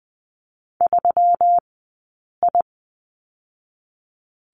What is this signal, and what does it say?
Morse code, heard as a keyed tone, about 20 wpm, 700 Hz. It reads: 3 I
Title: 217 Lesson 35d - Number 0 - Two Characters 20wpm